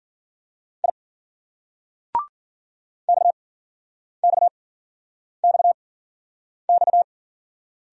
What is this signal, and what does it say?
Morse code, heard as a keyed tone, about 56 wpm, spaced as for 9 wpm, 700 Hz. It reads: I XXXX